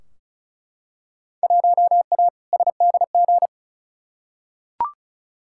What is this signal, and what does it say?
Morse code, heard as a keyed tone, about 36 wpm, 700 Hz. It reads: 1A SDG
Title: Field Day Exchanges Encore - 35wpm